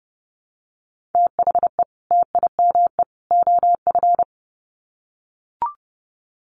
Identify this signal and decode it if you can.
Morse code, heard as a keyed tone, about 30 wpm, 700 Hz. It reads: THE TIME OF
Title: Sets of 3 Words 30wpm